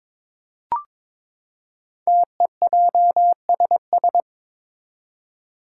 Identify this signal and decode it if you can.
Morse code, heard as a keyed tone, about 22 wpm, 700 Hz. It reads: TEJSS
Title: Code Groups - 5 Letters - 22wpm